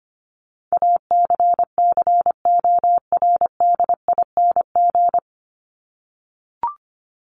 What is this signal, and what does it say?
Morse code, heard as a keyed tone, about 25 wpm, 700 Hz. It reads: ACCORDING
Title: LY Suffix - Top 100 Words with Base Words 25wpm